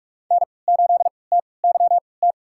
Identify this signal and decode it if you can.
Morse code, heard as a keyed tone, about 46 wpm, spaced as for 26 wpm, 700 Hz. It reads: N8TYT
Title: USA Call Signs - 5 Characters - 3x Character Spacing - 45wpm